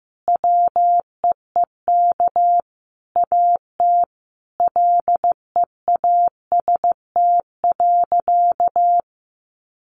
Morse code, heard as a keyed tone, about 15 wpm, 700 Hz. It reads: WEEK AT LEAST.
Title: Sentences from Top 900 Words 15wpm